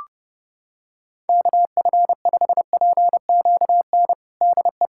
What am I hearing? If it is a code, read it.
Morse code, heard as a keyed tone, about 30 wpm, 700 Hz. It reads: KF5PQN DE
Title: USA Callsign DE Callsign - 6 Characters - 30wpm